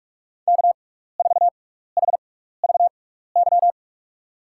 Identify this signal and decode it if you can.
Morse code, heard as a keyed tone, about 46 wpm, spaced as for 15 wpm, 700 Hz. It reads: K4HVY
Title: USA Call Signs - 5 Characters - 6x Character Spacing - 45wpm